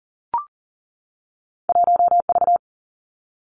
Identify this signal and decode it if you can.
Morse code, heard as a keyed tone, about 38 wpm, 700 Hz. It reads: 1V
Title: Two Challenging Letter-Number Combinations 40wpm